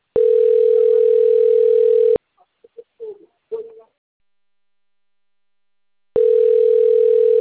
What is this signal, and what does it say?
Ringback tone (North America), 440 + 480 Hz.
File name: rg-1600-unknown- - - .69148